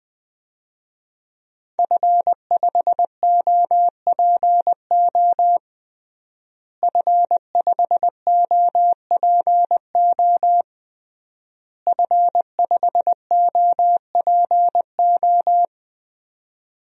Morse code, heard as a keyed tone, about 20 wpm, 700 Hz. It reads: F5OPO F5OPO F5OPO
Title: DXCC-100 Call Signs - 5 Characters - Repeated 3x - 20wpm